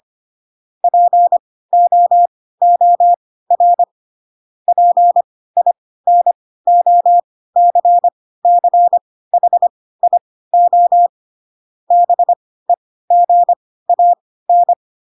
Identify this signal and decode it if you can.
Morse code, heard as a keyed tone, about 25 wpm, spaced as for 16 wpm, 700 Hz. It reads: POOR PINOCCHIO BEGAN
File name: Pinocchio / Pinocchio ch7 10 25 ik1hge